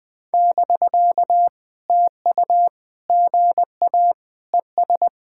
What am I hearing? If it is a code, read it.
Morse code, heard as a keyed tone, about 19 wpm, 700 Hz. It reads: <BK> TU GA ES
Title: POTA QSO Practice - 20wpm